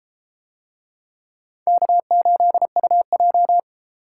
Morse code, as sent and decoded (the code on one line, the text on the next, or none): -.- ---.. ..- .---
K8UJ